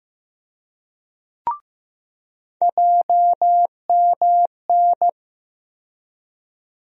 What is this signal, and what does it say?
Morse code, heard as a keyed tone, about 15 wpm, 700 Hz. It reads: JMN
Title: USA Call Sign Postfixes - 3 Letters 15wpm